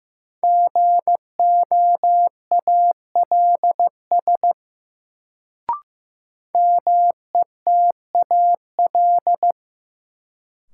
Morse code, heard as a keyed tone, about 16 wpm, 700 Hz. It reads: GOALS METAL
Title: Sentences from Top 1000 Words - Review Part 6 - New Word Drill - 15wpm